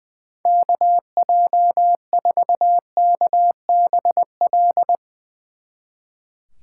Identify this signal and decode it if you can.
Morse code, heard as a keyed tone, about 21 wpm, 700 Hz. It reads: KJ4KBL
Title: Speed Racing - USA 2x3 Call Signs 20wpm